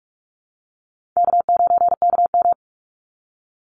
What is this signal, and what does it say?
Morse code, heard as a keyed tone, about 44 wpm, 700 Hz. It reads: K9KM
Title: Easy Call Signs Repeated 2x - 45wpm